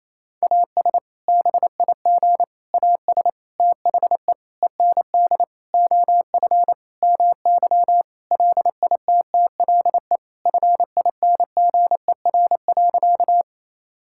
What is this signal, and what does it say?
Morse code, heard as a keyed tone, about 28 wpm, 700 Hz. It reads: AS BIG AS THE END OF MY LITTLE FINGER.
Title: Sentences from Top 1600 Words 28wpm